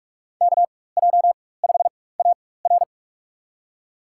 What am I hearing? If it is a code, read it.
Morse code, heard as a keyed tone, about 44 wpm, spaced as for 21 wpm, 690 Hz. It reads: KJ5AR
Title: USA Call Signs - 5 Characters - 4x Character Spacing - 45wpm